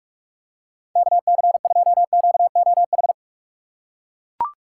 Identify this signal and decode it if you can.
Morse code, heard as a keyed tone, about 40 wpm, 700 Hz. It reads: KC2QOH